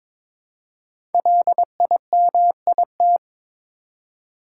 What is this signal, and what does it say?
Morse code, heard as a keyed tone, about 22 wpm, 700 Hz. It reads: LIMIT
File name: 5 Letter Words - First 3 Letters Given - 22wpm